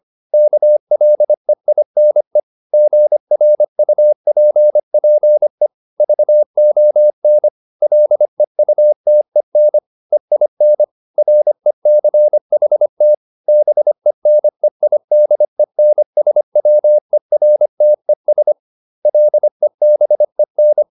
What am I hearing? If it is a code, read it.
Morse code, heard as a keyed tone, about 25 wpm, 600 Hz. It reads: KLEINE GRUPPE VON LEUTEN EIN RECHT BENEIDENSWERTES LEBEN